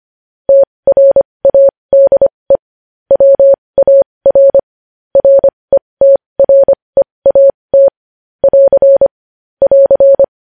Morse code, heard as a keyed tone, about 25 wpm, spaced as for 20 wpm, 550 Hz. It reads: TRADE WAR RETREAT <AR> <AR>
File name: podcast / News headlines-1x-25x20